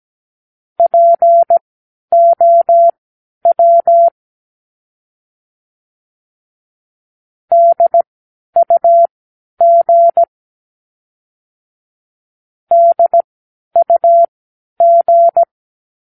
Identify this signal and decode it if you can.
Morse code, heard as a keyed tone, about 17 wpm, spaced as for 10 wpm, 680 Hz. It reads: POW DUG DUG
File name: k7qo / Practicewords